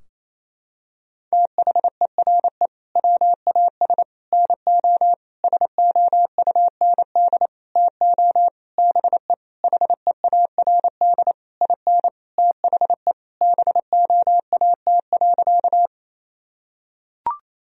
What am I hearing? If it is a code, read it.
Morse code, heard as a keyed tone, about 28 wpm, 700 Hz. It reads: THERE WAS NO SOUND TO BE HEARD IN THE BOAT.
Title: Sentences from Top 1000 Words - Review Part 3 - 28wpm